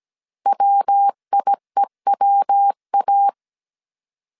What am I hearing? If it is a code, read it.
Morse code, heard as a keyed tone, about 17 wpm, 800 Hz. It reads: WIEWA